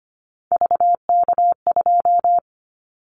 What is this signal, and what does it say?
Morse code, heard as a keyed tone, about 25 wpm, 700 Hz. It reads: VK2
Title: Callsign Prefixes for Top 100 DXCC - 25wpm